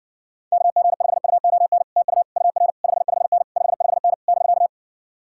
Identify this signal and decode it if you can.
Morse code, heard as a keyed tone, about 60 wpm, 700 Hz. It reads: KG5UON TU UR 55N 55N <BK>